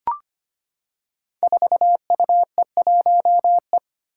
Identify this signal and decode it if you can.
Morse code, heard as a keyed tone, about 25 wpm, 700 Hz. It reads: 4UE1E
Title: Five Challenging Letter-Number Combinations 25wpm